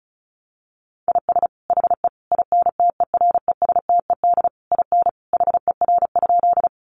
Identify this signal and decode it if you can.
Morse code, heard as a keyed tone, about 34 wpm, 700 Hz. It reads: IS HE INTERESTED IN HER?
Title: Sentences from Top 900 Words 35wpm